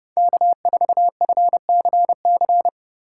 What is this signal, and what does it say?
Morse code, heard as a keyed tone, about 30 wpm, 700 Hz. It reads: K4FCC
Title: K1USN SST QSO Practice - 30wpm